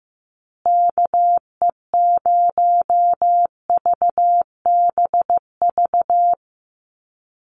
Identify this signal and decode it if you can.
Morse code, heard as a keyed tone, about 15 wpm, 700 Hz. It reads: KE0VBV
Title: USA Call Signs Repeated 3x 15wpm